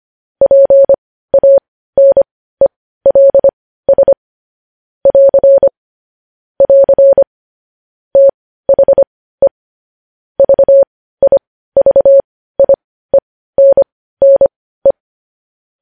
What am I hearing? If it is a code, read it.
Morse code, heard as a keyed tone, about 25 wpm, spaced as for 15 wpm, 550 Hz. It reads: PANELS <AR> <AR> THE VIVIENNE
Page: News headlines in morse code at 15 wpm